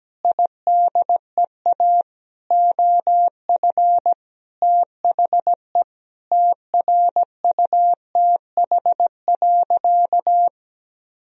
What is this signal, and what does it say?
Morse code, heard as a keyed tone, about 17 wpm, 700 Hz. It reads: IDEA OF THE TRUTH.
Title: Sentences from Top 1000 Words - Review Part 1 - 17wpm